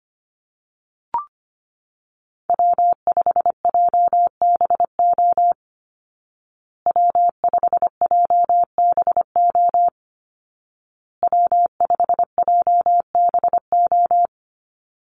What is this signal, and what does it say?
Morse code, heard as a keyed tone, about 25 wpm, 700 Hz. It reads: W5JBO W5JBO W5JBO
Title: Call Signs Repeated 2x - 25wpm